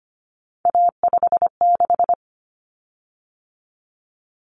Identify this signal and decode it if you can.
Morse code, heard as a keyed tone, about 25 wpm, 700 Hz. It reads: A56
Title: Common Cut Numbers - 3-Digit 25wpm